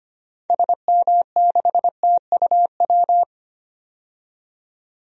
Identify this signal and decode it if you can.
Morse code, heard as a keyed tone, about 25 wpm, 700 Hz. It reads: SM6TUW
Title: DXCC-100 Call Signs - 6 Characters - 25wpm